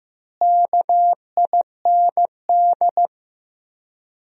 Morse code, heard as a keyed tone, about 15 wpm, 700 Hz. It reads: KIND
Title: 4x Word-Spacing - Sentences from Top 200 Words - 15wpm